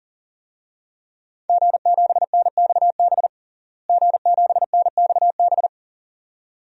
Morse code, heard as a keyed tone, about 40 wpm, 700 Hz. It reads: G7NXB G7NXB
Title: DXCC-100 Call Signs - 5 Characters - Repeated 2x - 40wpm